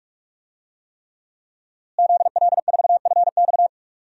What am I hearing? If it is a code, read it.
Morse code, heard as a keyed tone, about 46 wpm, 700 Hz. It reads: ZL4FX
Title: DXCC-100 Call Signs - 5 Characters - Repeated 3x - 45wpm